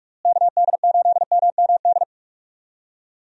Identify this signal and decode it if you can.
Morse code, heard as a keyed tone, about 46 wpm, 700 Hz. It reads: KD8MMD
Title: USA Call Signs - 6 Characters - Repeated 3x - 45wpm